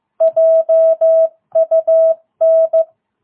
Morse code, heard as a keyed tone, about 14 wpm, 660 Hz. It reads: JUN
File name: jh8cbh / 2022cw